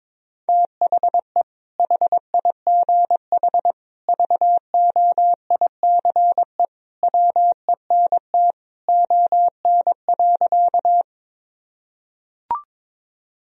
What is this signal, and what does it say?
Morse code, heard as a keyed tone, about 22 wpm, 700 Hz. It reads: THE HIGH VOICE WENT ON.
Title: Sentences from Top 1000 Words - Review Part 1 - 22wpm